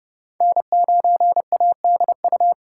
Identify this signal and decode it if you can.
Morse code, heard as a keyed tone, about 30 wpm, 700 Hz. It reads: N9ADU
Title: USA Call Signs - 5 Characters - 2x Character Spacing - 30wpm